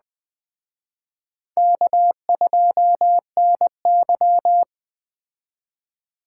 Morse code, heard as a keyed tone, about 20 wpm, 700 Hz. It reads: K2NY